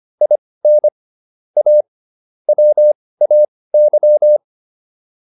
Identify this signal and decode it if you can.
Morse code, heard as a keyed tone, about 25 wpm, spaced as for 18 wpm, 600 Hz. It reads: IN A WAY